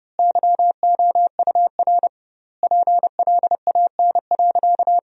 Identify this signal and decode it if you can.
Morse code, heard as a keyed tone, about 30 wpm, 700 Hz. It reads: YOUR PLAN.
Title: 2x Word-Spacing - Sentences from Top 500 Words - 30wpm